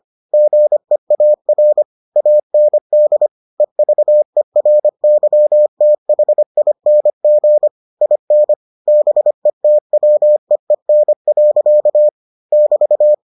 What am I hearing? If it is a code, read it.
Morse code, heard as a keyed tone, about 25 wpm, 600 Hz. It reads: GEAR AND EVERYTHING IN BETWEEN. <BT>